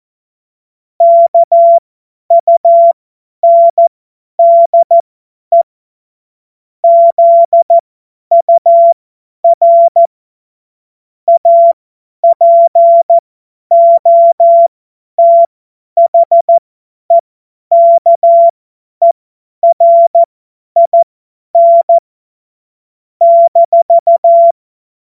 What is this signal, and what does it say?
Morse code, heard as a keyed tone, about 14 wpm, spaced as for 10 wpm, 680 Hz. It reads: KUNDE ZUR APOTHEKERIN -
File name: Morsekurs / Witze / Witz 30 Januar 2024 50 F70